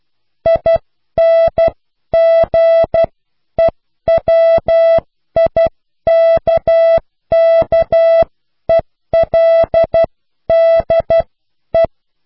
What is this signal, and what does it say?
Morse code, heard as a keyed tone, about 12 wpm, spaced as for 10 wpm, 660 Hz. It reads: INGEWIKKELDE